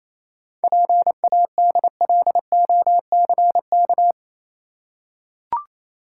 Morse code, heard as a keyed tone, about 27 wpm, 700 Hz. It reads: PADLOCK
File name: Easy Words - Speed-Racing with Hints - 22wpm